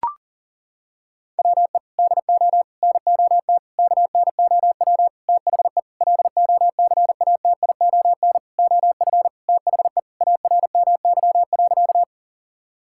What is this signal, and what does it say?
Morse code, heard as a keyed tone, about 40 wpm, 700 Hz. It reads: WE DO NOT KNOW THE LOCATION OF THE ARMY.
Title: Sentences from Top 1000 Words - Review Part 6 - 40wpm